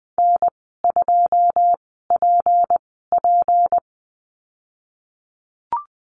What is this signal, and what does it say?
Morse code, heard as a keyed tone, about 20 wpm, spaced as for 15 wpm, 700 Hz. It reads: N2PP
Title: 193 Lesson 31g - Letter v - Call Signs 20wpm